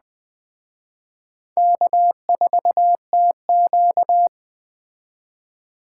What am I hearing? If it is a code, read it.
Morse code, heard as a keyed tone, about 21 wpm, 700 Hz. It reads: K4TQ
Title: Speed Racing - USA 1x2 Call Signs 17wpm